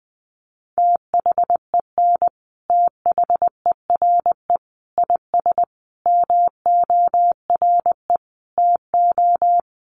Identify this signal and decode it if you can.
Morse code, heard as a keyed tone, about 20 wpm, 700 Hz. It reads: THEN THERE IS MORE TO
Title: Sentences from Top 100 Words 20wpm